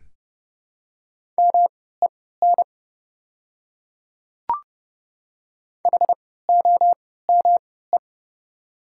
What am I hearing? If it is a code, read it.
Morse code, heard as a keyed tone, about 30 wpm, spaced as for 17 wpm, 700 Hz. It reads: MEN HOME
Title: 3x Character Spacing - Top 300 Words - 30wpm